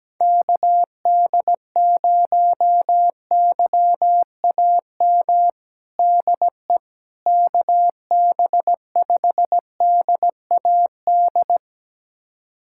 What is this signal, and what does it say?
Morse code, heard as a keyed tone, about 17 wpm, 700 Hz. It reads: KD0YAM DE KB5DAD
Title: USA Callsign DE Callsign - 6 Characters - 17wpm